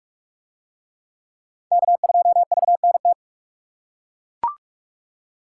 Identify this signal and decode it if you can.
Morse code, heard as a keyed tone, about 39 wpm, 700 Hz. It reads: K2VNT